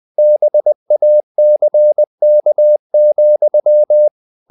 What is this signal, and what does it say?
Morse code, heard as a keyed tone, about 20 wpm, 600 Hz. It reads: BACK,
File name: the old man and the sea part 2 20wpm